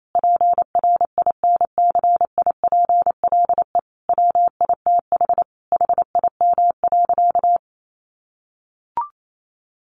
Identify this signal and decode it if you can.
Morse code, heard as a keyed tone, about 28 wpm, 700 Hz. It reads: PRINCIPLE WITH HIM.
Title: Sentences from Top 1400 Words 28wpm